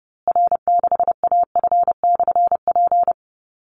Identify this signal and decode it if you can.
Morse code, heard as a keyed tone, about 30 wpm, 700 Hz. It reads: R6AF/P
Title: Call Signs Repeated 3x - 30wpm